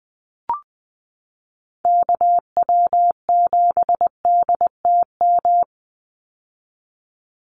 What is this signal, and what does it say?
Morse code, heard as a keyed tone, about 20 wpm, 700 Hz. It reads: KW7DTM
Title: USA Call Signs 20wpm